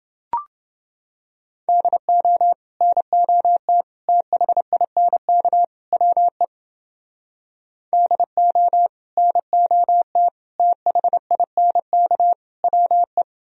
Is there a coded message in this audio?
Yes: Morse code from 1.7 to 13.2 s, heard as a keyed tone, about 30 wpm, 700 Hz. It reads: DO NOT THINK WE DO NOT THINK WE